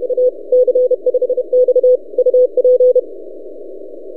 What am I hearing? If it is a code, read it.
Morse code, heard as a keyed tone, about 31 wpm, 500 Hz. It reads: A C5X UP
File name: posts / c5x-30cw- -0600z